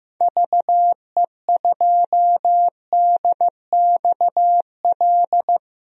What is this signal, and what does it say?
Morse code, heard as a keyed tone, about 15 wpm, 690 Hz. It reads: VE2DXL
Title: DXCC-100 Call Signs - 6 Characters - 15wpm